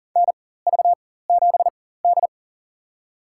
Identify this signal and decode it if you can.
Morse code, heard as a keyed tone, about 40 wpm, spaced as for 19 wpm, 700 Hz. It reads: NV7D
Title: USA Call Signs - 4 Characters - 4x Character Spacing - 40wpm